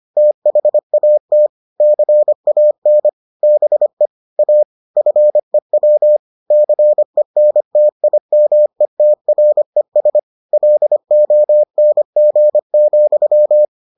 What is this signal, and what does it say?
Morse code, heard as a keyed tone, about 25 wpm, 600 Hz. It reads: THAT CAN BE A FEW CENTIMETRES LONG,